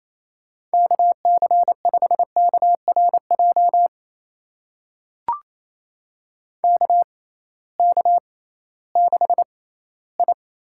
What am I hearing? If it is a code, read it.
Morse code, heard as a keyed tone, about 28 wpm, spaced as for 16 wpm, 700 Hz. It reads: KC5KRJ KK6I
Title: USA Call Signs - 6 Characters - 6x Character Spacing - 28wpm